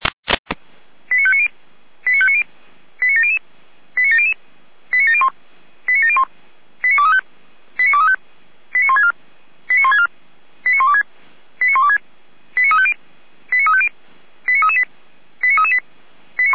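Five-tone selective call (ZVEI), heard as quick runs of five tones, tones of about 70 ms. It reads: 89590 89590 89700 89700 89701 89701 89235 89235 89164 89164 89126 89126 89350 89350 89338 89338
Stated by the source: ZVEI five-tone selective call 89092 received on an SDR.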